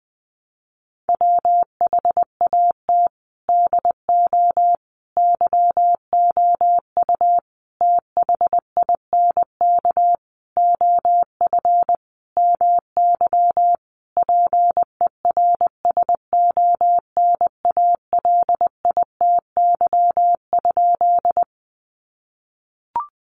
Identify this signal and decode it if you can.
Morse code, heard as a keyed tone, about 20 wpm, 700 Hz. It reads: WHAT DO YOU THINK OF MY PERSONALITY?
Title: Sentences from Top 1400 Words 20wpm